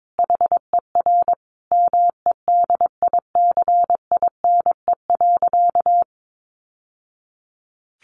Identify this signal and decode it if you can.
Morse code, heard as a keyed tone, about 22 wpm, 700 Hz. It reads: HER MEDICINE.
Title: Sentences from Top 900 Words 22wpm